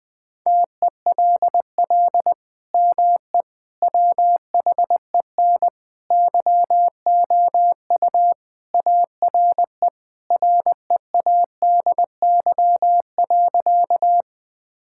Morse code, heard as a keyed tone, about 21 wpm, 700 Hz. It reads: TELL ME WHEN YOU ARE READY.
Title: Sentences with 2-Word Phrases - Spoken Beginning - 20wpm